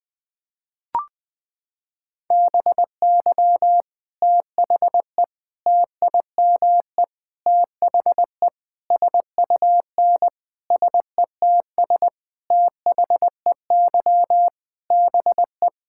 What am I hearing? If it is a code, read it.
Morse code, heard as a keyed tone, about 21 wpm, 700 Hz. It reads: BY THE TIME THE SUN SETS THEY BE
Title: Sentences with 4-Word Phrases - 20wpm